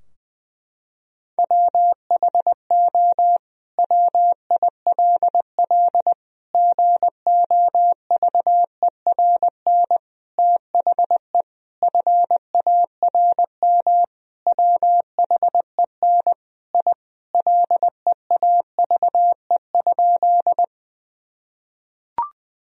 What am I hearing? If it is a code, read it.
Morse code, heard as a keyed tone, about 20 wpm, 700 Hz. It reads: WHO WILL GOVERN THE FARM WHEN I LEAVE?
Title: Sentences from Top 1000 Words - Review Part 2 - 20wpm